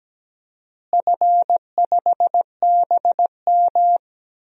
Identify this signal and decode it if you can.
Morse code, heard as a keyed tone, about 17 wpm, 700 Hz. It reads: F5BM
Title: DXCC-100 Call Signs - 4 Characters - Repeated 3x - 17wpm